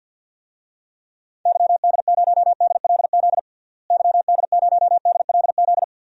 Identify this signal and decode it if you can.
Morse code, heard as a keyed tone, about 53 wpm, 700 Hz. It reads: YD0DLZ YD0DLZ
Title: DXCC-100 Call Signs - 6 Characters - Repeated 3x - 50wpm